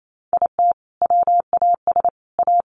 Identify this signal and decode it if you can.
Morse code, heard as a keyed tone, about 28 wpm, 700 Hz. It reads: IT WAS A
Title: Sentences from Top 1100 Words 28wpm